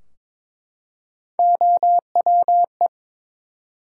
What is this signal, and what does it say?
Morse code, heard as a keyed tone, about 23 wpm, 700 Hz. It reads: OWE
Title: Three Letter Words - 22wpm